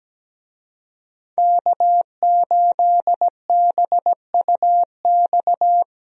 Morse code, heard as a keyed tone, about 17 wpm, 700 Hz. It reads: K8BUX
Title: USA Call Signs - 5 Characters - Repeated 3x - 17wpm